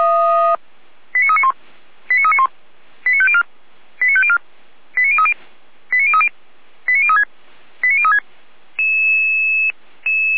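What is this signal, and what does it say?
Five-tone selective call (ZVEI), heard as quick runs of five tones, tones of about 80 ms. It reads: 89391 89391 89504 89504 89030 89030 89036 89036
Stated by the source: ZVEI five-tone selective call received via SDR.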